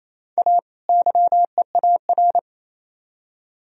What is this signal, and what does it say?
Morse code, heard as a keyed tone, about 34 wpm, 700 Hz. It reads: A YEAR
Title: Speed-Racing - Sets of 2 Words 28wpm